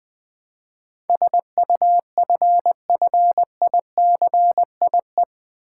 Morse code, heard as a keyed tone, about 20 wpm, 700 Hz. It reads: SUFFICIE
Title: Build A Word - 4 Letters At A Time - 12-Letter Words - 20wpm